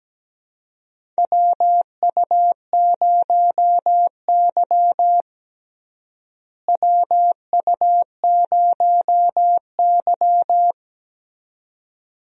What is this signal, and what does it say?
Morse code, heard as a keyed tone, about 17 wpm, 700 Hz. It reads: WU0Y WU0Y
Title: USA Call Signs - 4 Characters - Repeated 2x - 17wpm